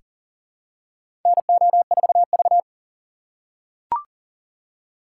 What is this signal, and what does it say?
Morse code, heard as a keyed tone, about 40 wpm, 700 Hz. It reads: NO4V
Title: USA Call Signs - 4 Characters - 4x Character Spacing - 40wpm